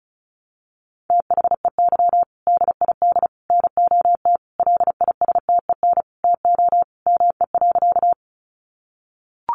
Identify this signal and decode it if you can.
Morse code, heard as a keyed tone, about 34 wpm, 700 Hz. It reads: THEY DID NOT LISTEN TO ME.
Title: Sentences from Top 600 Words 35wpm